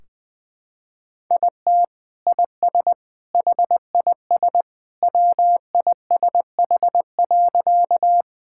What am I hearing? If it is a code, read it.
Morse code, heard as a keyed tone, about 20 wpm, 700 Hz. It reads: IT IS HIS WISH.
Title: Sentences from Top 1600 Words 20wpm